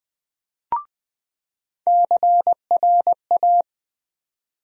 Morse code, heard as a keyed tone, about 21 wpm, 700 Hz. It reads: CRA
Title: USA Call Sign Postfixes - 3 Letters 20wpm